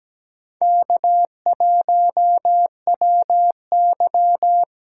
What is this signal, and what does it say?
Morse code, heard as a keyed tone, about 20 wpm, 700 Hz. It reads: K1WY
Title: Speed Racing - USA 1x2 Call Signs 17wpm